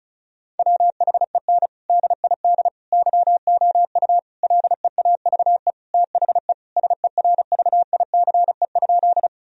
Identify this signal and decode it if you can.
Morse code, heard as a keyed tone, about 36 wpm, 700 Hz. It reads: WHEN DID YOU LEAVE THE SERVICE?
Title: Sentences from Top 1000 Words - Review Part 4 - 35wpm